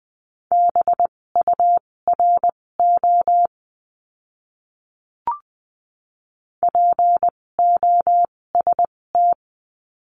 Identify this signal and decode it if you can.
Morse code, heard as a keyed tone, about 20 wpm, spaced as for 16 wpm, 700 Hz. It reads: BURO POST
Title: 256 Lesson 42a - Words 12-20wpm